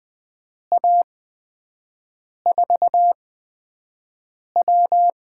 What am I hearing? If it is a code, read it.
Morse code, heard as a keyed tone, about 21 wpm, spaced as for 5 wpm, 700 Hz. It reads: A4W
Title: USA Call Signs - 6 Characters - 8x Character Spacing - 20wpm